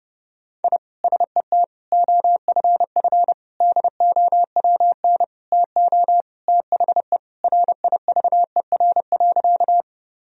Morse code, heard as a keyed tone, about 30 wpm, 700 Hz. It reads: I SET OFF DOWN TO THE RIVER.
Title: Sentences from Top 1000 Words - Review Part 1 - 30wpm